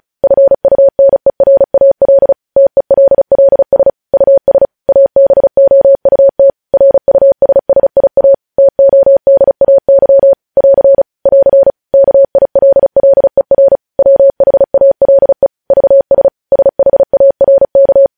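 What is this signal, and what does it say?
Morse code, heard as a keyed tone, about 36 wpm, 550 Hz. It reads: FUNERAL TELLS US ABOUT RUSSIA TODAY <AR> <AR> KILLER WHALE VS SHARK
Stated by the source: Morse practice at 35 wpm sending news headlines.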